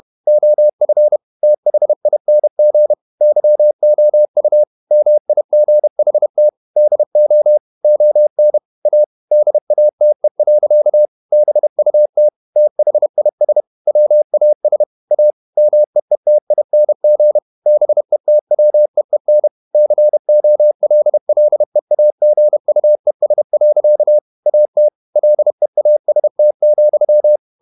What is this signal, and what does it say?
Morse code, heard as a keyed tone, about 31 wpm, 600 Hz. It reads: OF THING YOU MIGHT DO ON A DATE. BUT THIS WAS A MEETING BETWEEN COLLEAGUES. AT LEAST,
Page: Never 31wpm | CW med Gnister